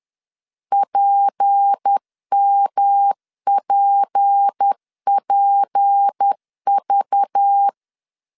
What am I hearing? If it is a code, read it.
Morse code, heard as a keyed tone, about 10 wpm, 800 Hz. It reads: PMPPV
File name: LEZIONI / 6 PKX0 / 6 P 10wpm 2432